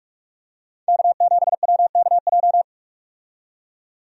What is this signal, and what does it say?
Morse code, heard as a keyed tone, about 46 wpm, 700 Hz. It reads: K7WKJ
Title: USA Call Signs - 5 Characters - 45wpm